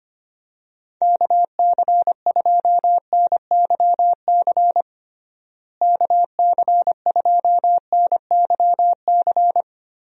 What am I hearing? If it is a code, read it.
Morse code, heard as a keyed tone, about 25 wpm, 700 Hz. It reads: KC2NYC KC2NYC
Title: USA Call Signs - 6 Characters - Repeated 3x - 25wpm